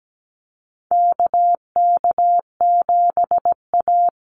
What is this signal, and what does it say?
Morse code, heard as a keyed tone, about 17 wpm, 700 Hz. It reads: KK7A
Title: Call Signs Repeated 2x - 17wpm